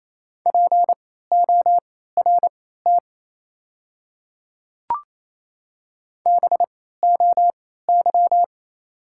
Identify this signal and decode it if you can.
Morse code, heard as a keyed tone, about 28 wpm, spaced as for 16 wpm, 690 Hz. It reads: PORT BOY
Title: 3x Character Spacing - Top 300 Words - 28wpm